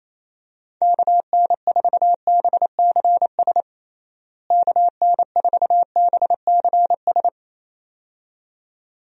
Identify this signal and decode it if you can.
Morse code, heard as a keyed tone, about 27 wpm, 700 Hz. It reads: KN4BCS KN4BCS
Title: USA Call Signs - 6 Characters - Repeated 2x - 28wpm